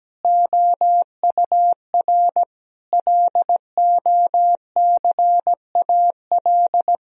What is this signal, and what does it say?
Morse code, heard as a keyed tone, about 17 wpm, 700 Hz. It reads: OUR LOCAL
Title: Sentences from Top 1000 Words - Review Part 4 - 17wpm